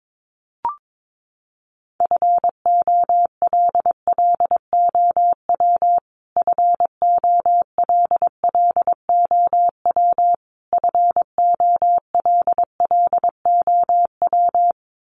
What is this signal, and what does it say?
Morse code, heard as a keyed tone, about 22 wpm, 700 Hz. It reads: FOLLOW FOLLOW FOLLOW
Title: Top 200 Words Repeated 22wpm